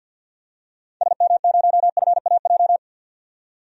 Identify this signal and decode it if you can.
Morse code, heard as a keyed tone, about 52 wpm, 700 Hz. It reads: SM0FAJ
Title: DXCC-100 Call Signs - 6 Characters - 50wpm